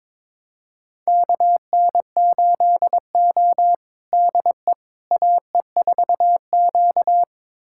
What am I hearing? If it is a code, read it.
Morse code, heard as a keyed tone, about 21 wpm, 700 Hz. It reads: KN8O DE AE4Q
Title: USA Callsign DE Callsign - 4 Characters - 22wpm